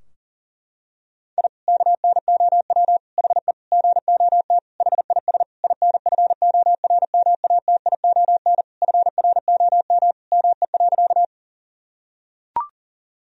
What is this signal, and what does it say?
Morse code, heard as a keyed tone, about 40 wpm, 700 Hz. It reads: I KNOW HE GOT HIS INFORMATION FROM ME.
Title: Sentences from Top 1000 Words - Review Part 3 - 40wpm